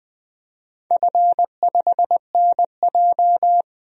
Morse code, heard as a keyed tone, about 20 wpm, 700 Hz. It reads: F5NJ
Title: DXCC-100 Call Signs - 4 Characters - 20wpm